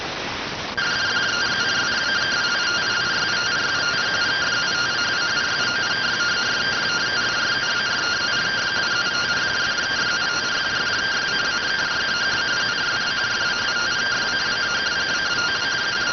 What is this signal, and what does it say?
Two-tone teleprinter signal, about 170 Hz shift.